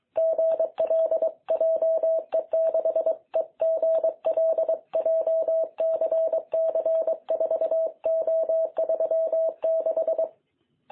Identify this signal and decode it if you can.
Morse code, heard as a keyed tone, about 22 wpm, 660 Hz. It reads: GL JE6EGL JCC4O36
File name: jh8cbh / 2021cw